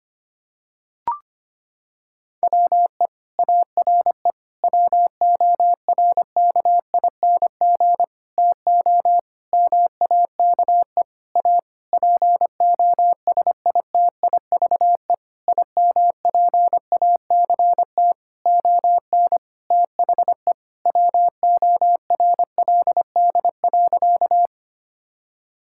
Morse code, heard as a keyed tone, about 25 wpm, 700 Hz. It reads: WE ARE WORKING TO MAKE A POSITIVE IMPACT ON THE WORLD.
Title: Sentences from Top 1000 Words - Review Part 6 - 25wpm